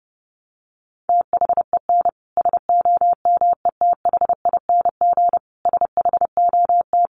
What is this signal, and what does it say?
Morse code, heard as a keyed tone, about 28 wpm, 700 Hz. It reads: THEN SOMETHING SHOT
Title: Sentences from Top 1100 Words 30wpm